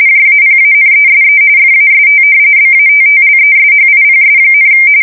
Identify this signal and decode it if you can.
RTTY, about 45.45 baud. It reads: JAPANESE FAST BATTLESHIPS,
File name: fsk170 / battleshipnj